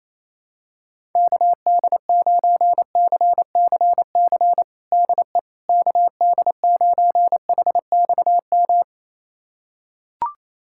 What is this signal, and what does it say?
Morse code, heard as a keyed tone, about 28 wpm, 700 Hz. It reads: KD9CCC DE KD9HXM
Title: USA Callsign DE Callsign - 6 Characters - 28wpm